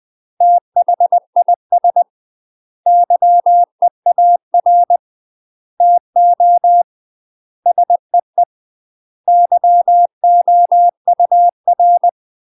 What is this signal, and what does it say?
Morse code, heard as a keyed tone, about 20 wpm, 700 Hz. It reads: THIS YEAR TO SEE YOUR
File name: UK MASTER QSOs TRAINING / 20WPM / UK MASTER QSOs 0014